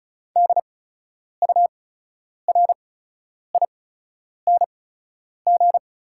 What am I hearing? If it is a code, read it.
Morse code, heard as a keyed tone, about 36 wpm, spaced as for 10 wpm, 700 Hz. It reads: DURING
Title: 8x Character Spacing - Top 500 Words - 35wpm